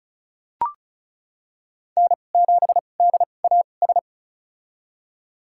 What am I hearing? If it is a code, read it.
Morse code, heard as a keyed tone, about 36 wpm, spaced as for 25 wpm, 700 Hz. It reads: N7DAS
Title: USA Call Signs - 5 Characters - 2x Character Spacing - 35wpm